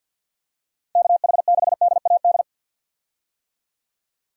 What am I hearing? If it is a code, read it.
Morse code, heard as a keyed tone, about 52 wpm, 700 Hz. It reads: KH6DAD
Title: DXCC-100 Call Signs - 6 Characters - 50wpm